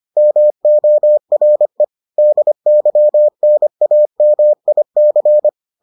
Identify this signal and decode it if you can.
Morse code, heard as a keyed tone, about 25 wpm, 600 Hz. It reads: MORE DYNAMIC